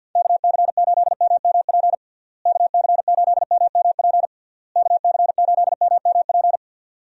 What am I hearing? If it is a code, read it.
Morse code, heard as a keyed tone, about 50 wpm, 700 Hz. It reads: KC8MMP KC8MMP KC8MMP
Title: USA Call Signs - 6 Characters - Repeated 3x - 50wpm